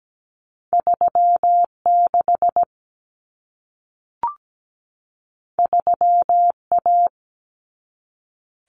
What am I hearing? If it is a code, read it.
Morse code, heard as a keyed tone, about 17 wpm, 700 Hz. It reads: 36 3A
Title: Common Cut Numbers - 2-Digit 17wpm